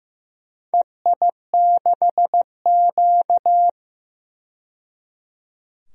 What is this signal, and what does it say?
Morse code, heard as a keyed tone, about 15 wpm, 700 Hz. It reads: EI6Q
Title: DXCC-100 Call Signs - 4 Characters - 15wpm